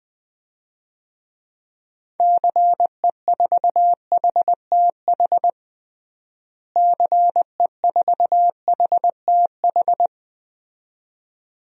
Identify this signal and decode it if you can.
Morse code, heard as a keyed tone, about 21 wpm, 690 Hz. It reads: CE4HTH CE4HTH
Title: DXCC-100 Call Signs - 6 Characters - Repeated 2x - 20wpm